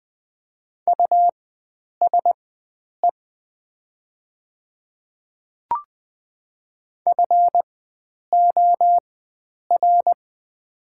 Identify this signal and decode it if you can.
Morse code, heard as a keyed tone, about 20 wpm, spaced as for 9 wpm, 700 Hz. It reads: USE FOR
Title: 4x Character Spacing - Top 100 Words - 20wpm